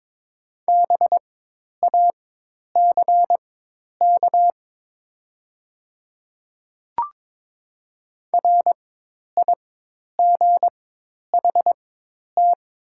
Morse code, heard as a keyed tone, about 22 wpm, spaced as for 10 wpm, 700 Hz. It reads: BACK RIGHT
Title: 4x Character Spacing - Top 200 Words - 22wpm